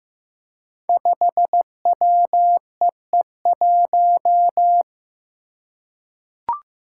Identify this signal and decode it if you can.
Morse code, heard as a keyed tone, about 15 wpm, 700 Hz. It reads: 5WEE1
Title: Five Challenging Letter-Number Combinations 15wpm